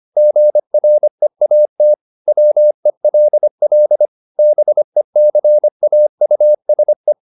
Morse code, heard as a keyed tone, about 25 wpm, 600 Hz. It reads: GREAT WELL BECAUSE
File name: the old man and the sea part 1 25wpm